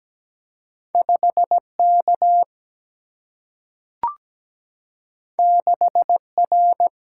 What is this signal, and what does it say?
Morse code, heard as a keyed tone, about 17 wpm, 700 Hz. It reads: 5K 6R
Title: Single-Letter Single-Number 17wpm